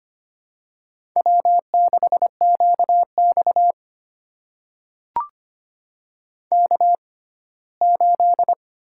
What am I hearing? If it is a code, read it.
Morse code, heard as a keyed tone, about 25 wpm, spaced as for 17 wpm, 700 Hz. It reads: W6QX K8
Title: USA Call Signs - 4 Characters - 6x Character Spacing - 25wpm